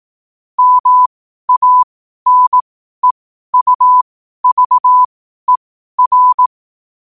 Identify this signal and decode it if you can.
Morse code, heard as a keyed tone, about 18 wpm, spaced as for 12 wpm, 1000 Hz. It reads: MANEUVER
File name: qotdfiles / 12wpm